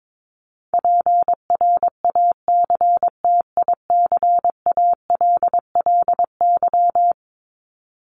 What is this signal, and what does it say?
Morse code, heard as a keyed tone, about 22 wpm, 700 Hz. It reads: PRACTICALLY
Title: LY Suffix - Top 100 Words with Base Words 22wpm